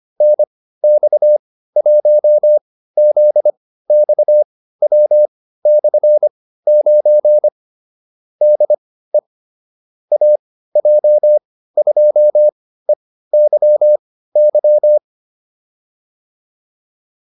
Morse code, heard as a keyed tone, about 25 wpm, spaced as for 15 wpm, 600 Hz. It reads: NX1ZXW/9 DE AJ2EYY
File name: QSO307 15